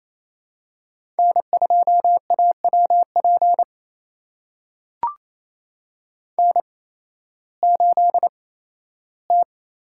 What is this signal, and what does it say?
Morse code, heard as a keyed tone, about 28 wpm, spaced as for 15 wpm, 700 Hz. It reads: N2AWP N8T
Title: USA Call Signs - 5 Characters - 8x Character Spacing - 28wpm